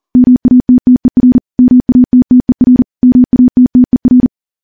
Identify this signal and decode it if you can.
Morse code, heard as a keyed tone, about 40 wpm, 250 Hz. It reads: MATTER MATTER MATTER
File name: morse / morse-training-40WPM